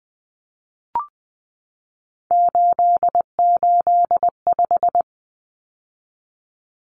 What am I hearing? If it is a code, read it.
Morse code, heard as a keyed tone, about 20 wpm, 700 Hz. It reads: 885
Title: Numbers - 3-Digit 20wpm